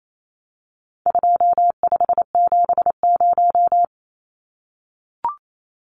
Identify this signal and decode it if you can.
Morse code, heard as a keyed tone, about 28 wpm, 700 Hz. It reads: 2570
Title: Numbers - 4-Digit 28wpm